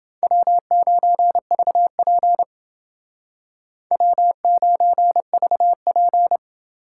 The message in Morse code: W9VP W9VP

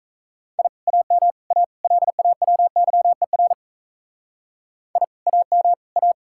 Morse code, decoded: I AM A LAWYER I AM A

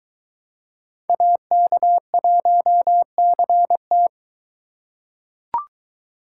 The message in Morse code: AK1CT

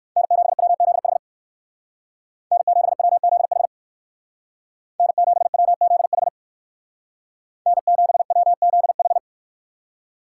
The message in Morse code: N7WZH N7WZH N7WZH N7WZH

